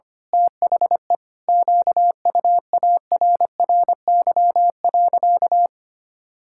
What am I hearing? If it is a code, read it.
Morse code, heard as a keyed tone, about 25 wpm, 700 Hz. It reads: THE QUARRY.
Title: Pangram Challenge v2 - 25wpm